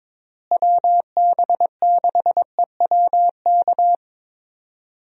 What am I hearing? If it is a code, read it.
Morse code, heard as a keyed tone, about 22 wpm, 700 Hz. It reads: WB6EWK
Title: USA Call Signs - 6 Characters - Repeated 3x - 22wpm